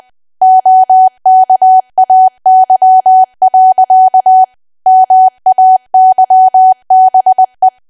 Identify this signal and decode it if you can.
Morse code, heard as a keyed tone, about 20 wpm, 750 Hz. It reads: OKAY. MAYBE
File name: cwa / W1AW-Level3 / 20WPM